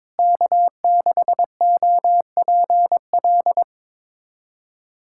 Morse code, heard as a keyed tone, about 21 wpm, 700 Hz. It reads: K6OPL
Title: USA Callsign DE Callsign - 5 Characters - 22wpm